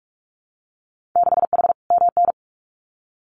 Morse code, heard as a keyed tone, about 46 wpm, 700 Hz. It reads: 6H MN
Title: Winter Field Day Exchanges Encore 45wpm